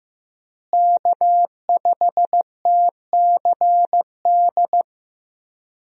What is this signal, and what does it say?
Morse code, heard as a keyed tone, about 15 wpm, 700 Hz. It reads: K5TCD
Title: USA Call Signs - 5 Characters - Repeated 2x - 15wpm